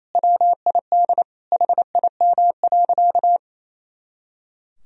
Morse code, heard as a keyed tone, about 28 wpm, 700 Hz. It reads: WID HIM.
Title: Sentences from Top 300 Words and Abbreviations 28wpm